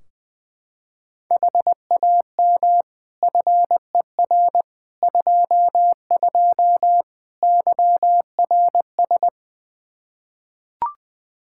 Morse code, heard as a keyed tone, about 20 wpm, 700 Hz. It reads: HAM FER 22 YRS
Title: Practice QSO - Exchange 3 - More Info - 20wpm